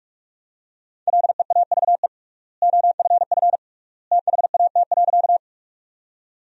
Morse code, heard as a keyed tone, about 46 wpm, 700 Hz. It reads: LEAVE OFF THAT.
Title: 3x Word-Spacing - Sentences from Top 400 Words - 45wpm